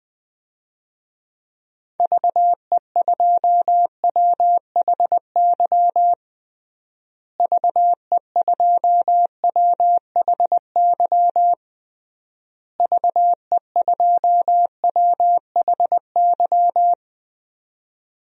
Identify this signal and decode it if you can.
Morse code, heard as a keyed tone, about 20 wpm, 700 Hz. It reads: VE2WHY VE2WHY VE2WHY
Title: DXCC-100 Call Signs - 6 Characters - Repeated 3x - 20wpm